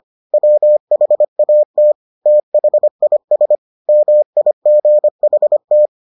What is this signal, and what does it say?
Morse code, heard as a keyed tone, about 25 wpm, 600 Hz. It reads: WHAT THIS MIGHT